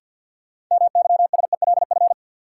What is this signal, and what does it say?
Morse code, heard as a keyed tone, about 47 wpm, 700 Hz. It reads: MYSELF
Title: Sentences from Top 1300 Words - New Word Drill - 50wpm